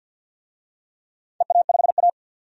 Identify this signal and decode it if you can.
Morse code, heard as a keyed tone, about 52 wpm, 700 Hz. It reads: EA5A
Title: DXCC-100 Call Signs - 4 Characters - 50wpm